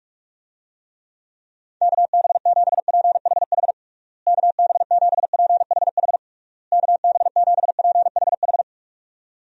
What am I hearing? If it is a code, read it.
Morse code, heard as a keyed tone, about 46 wpm, 700 Hz. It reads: KB7PHH KB7PHH KB7PHH
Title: USA Call Signs - 6 Characters - Repeated 3x - 45wpm